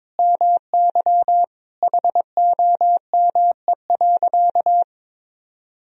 Morse code, heard as a keyed tone, about 22 wpm, 700 Hz. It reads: MY HOME.
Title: Sentences from Top 1000 Words - Review Part 2 - 22wpm